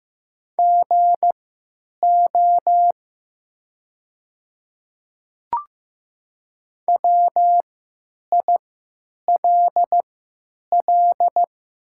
Morse code, heard as a keyed tone, about 15 wpm, spaced as for 8 wpm, 700 Hz. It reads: GO WILL
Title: 3x Character Spacing - Top 200 Words - 15wpm